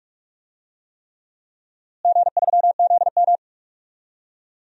Morse code, heard as a keyed tone, about 46 wpm, 700 Hz. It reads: G3ZM
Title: DXCC-100 Call Signs - 4 Characters - Repeated 3x - 45wpm